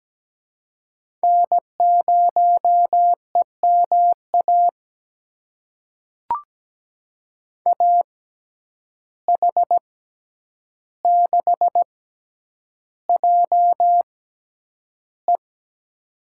Morse code, heard as a keyed tone, about 17 wpm, spaced as for 9 wpm, 700 Hz. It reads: N0EMA AH6JE
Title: USA Call Signs - 5 Characters - 6x Character Spacing - 17wpm